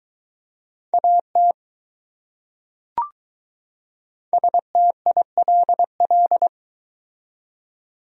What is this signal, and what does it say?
Morse code, heard as a keyed tone, about 21 wpm, 700 Hz. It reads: AT STILL